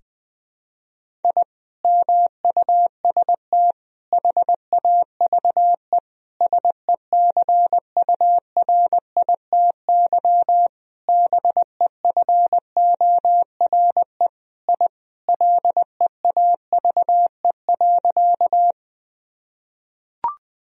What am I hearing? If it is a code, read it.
Morse code, heard as a keyed tone, about 20 wpm, 700 Hz. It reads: I MUST HAVE SECURITY BEFORE I LEAVE.
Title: Sentences from Top 1000 Words - Review Part 4 - 20wpm